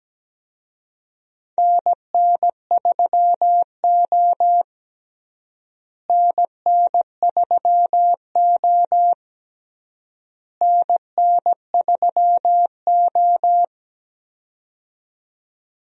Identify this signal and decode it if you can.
Morse code, heard as a keyed tone, about 17 wpm, 700 Hz. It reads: NN3O NN3O NN3O
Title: USA Call Signs - 4 Characters - Repeated 3x - 17wpm